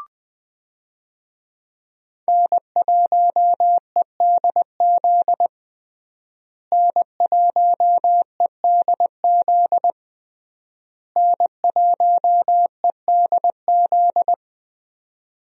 Morse code, heard as a keyed tone, about 20 wpm, 700 Hz. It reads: N1EDZ N1EDZ N1EDZ
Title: USA Call Signs - 5 Characters - Repeated 3x - 20wpm